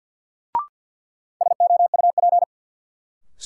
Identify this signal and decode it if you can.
Morse code, heard as a keyed tone, about 47 wpm, 700 Hz. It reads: SOUP
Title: Sentences from Top 1000 Words - Review Part 6 - New Word Drill - 50wpm